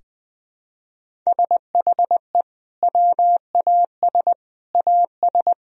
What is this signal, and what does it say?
Morse code, heard as a keyed tone, about 21 wpm, 700 Hz. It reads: SHE WAS AS
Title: Sentences from Top 1000 Words - Review Part 4 - 20wpm